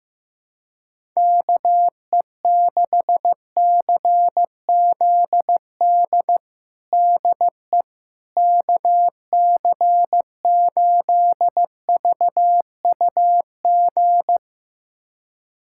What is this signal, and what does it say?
Morse code, heard as a keyed tone, about 16 wpm, 700 Hz. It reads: KE6CZD DE KC8VUG
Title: USA Callsign DE Callsign - 6 Characters - 15wpm